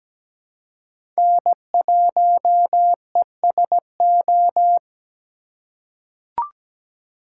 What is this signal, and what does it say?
Morse code, heard as a keyed tone, about 17 wpm, 700 Hz. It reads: N1ESO
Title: USA Call Signs - 5 Characters - 2x Character Spacing - 17wpm